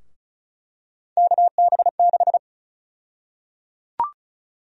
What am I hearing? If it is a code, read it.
Morse code, heard as a keyed tone, about 34 wpm, 700 Hz. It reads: KB6
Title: USA Call Sign Prefixes - 35wpm